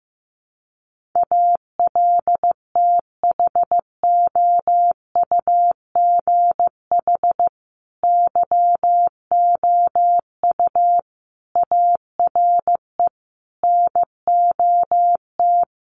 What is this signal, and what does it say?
Morse code, heard as a keyed tone, about 14 wpm, 700 Hz. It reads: ALTHOUGH YOU ARE NOT
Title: Sentences from Top 1100 Words 15wpm